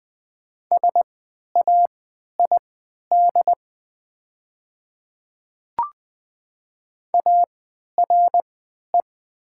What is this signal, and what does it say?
Morse code, heard as a keyed tone, about 20 wpm, spaced as for 11 wpm, 700 Hz. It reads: SAID ARE
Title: 3x Character Spacing - Top 300 Words - 20wpm